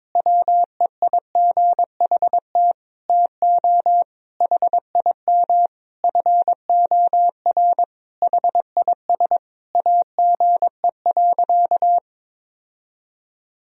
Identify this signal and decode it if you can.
Morse code, heard as a keyed tone, about 22 wpm, 700 Hz. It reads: WEIGHT TO HIM FOR HIS AGE.
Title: Sentences from Top 1000 Words - Review Part 3 - 22wpm